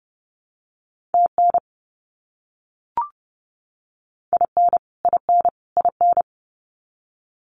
Morse code, heard as a keyed tone, about 30 wpm, 700 Hz. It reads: TN IN IN IN
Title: US States Repeated 3x 30wpm